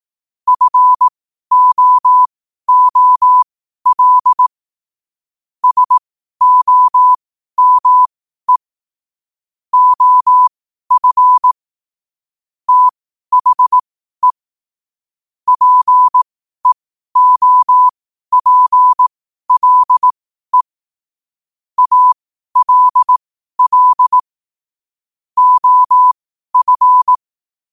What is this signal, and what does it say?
Morse code, heard as a keyed tone, about 18 wpm, spaced as for 12 wpm, 1000 Hz. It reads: FOOL SOME OF THE PEOPLE ALL OF
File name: qotdfiles / 12wpm